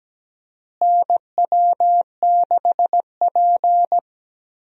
Morse code, heard as a keyed tone, about 17 wpm, 700 Hz. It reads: NW6P
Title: USA Call Signs - 4 Characters - Repeated 3x - 17wpm